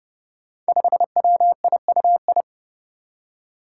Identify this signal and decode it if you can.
Morse code, heard as a keyed tone, about 30 wpm, 700 Hz. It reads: 5WIUI
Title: Five Challenging Letter-Number Combinations 30wpm